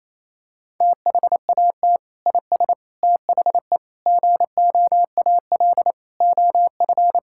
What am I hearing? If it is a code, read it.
Morse code, heard as a keyed tone, about 27 wpm, 700 Hz. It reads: THAT IS THE GOAL OF
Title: Sentences from Top 1000 Words - Review Part 5 - 28wpm